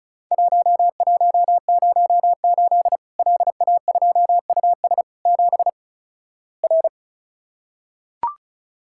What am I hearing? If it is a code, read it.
Morse code, heard as a keyed tone, about 36 wpm, 700 Hz. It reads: 1108 LA2US 7 R
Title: Worked All Europe DX Contest - QTC Exchange - 35wpm